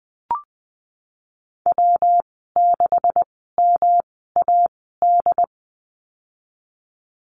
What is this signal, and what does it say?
Morse code, heard as a keyed tone, about 20 wpm, spaced as for 15 wpm, 700 Hz. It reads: W6MAD
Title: 128 Lesson 21g - Number 6 - Call Signs 20wpm